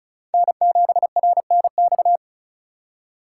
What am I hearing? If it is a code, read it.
Morse code, heard as a keyed tone, about 34 wpm, 700 Hz. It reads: N7RNX
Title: USA Call Signs - 5 Characters - 3x Character Spacing - 35wpm